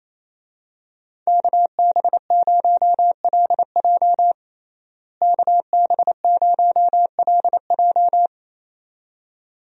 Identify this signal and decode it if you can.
Morse code, heard as a keyed tone, about 28 wpm, 700 Hz. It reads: KB0LJ KB0LJ
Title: USA Call Signs - 5 Characters - Repeated 2x - 28wpm